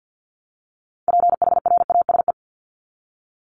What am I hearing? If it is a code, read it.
Morse code, heard as a keyed tone, about 48 wpm, 700 Hz. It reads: PHRASE
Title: Top 1501-1750 Words 50wpm